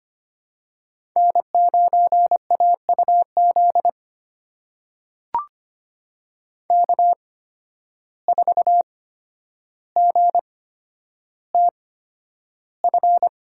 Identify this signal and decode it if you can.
Morse code, heard as a keyed tone, about 25 wpm, spaced as for 11 wpm, 700 Hz. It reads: N9AUZ K4GTF
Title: USA Call Signs - 5 Characters - 8x Character Spacing - 25wpm